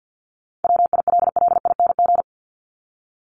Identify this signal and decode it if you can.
Morse code, heard as a keyed tone, about 50 wpm, 700 Hz. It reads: WILLING
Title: Top 751-1000 Words 50wpm